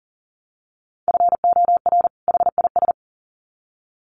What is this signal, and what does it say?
Morse code, heard as a keyed tone, about 39 wpm, 700 Hz. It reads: FOR HIS
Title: Sets of 2 Words 40wpm